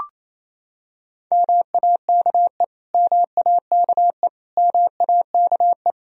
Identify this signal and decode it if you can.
Morse code, heard as a keyed tone, about 28 wpm, 700 Hz. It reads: MAKE MAKE MAKE
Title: Top 100 Words Repeated 28wpm